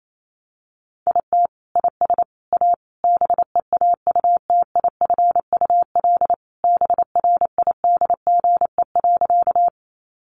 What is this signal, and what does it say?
Morse code, heard as a keyed tone, about 28 wpm, 700 Hz. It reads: IT IS A BEAUTIFUL BRIDGE.
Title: Sentences from Top 1200 Words 28wpm